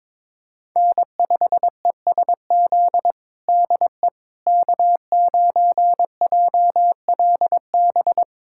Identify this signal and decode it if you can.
Morse code, heard as a keyed tone, about 22 wpm, 700 Hz. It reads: N5ESZ DE K9JLB
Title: USA Callsign DE Callsign - 5 Characters - 22wpm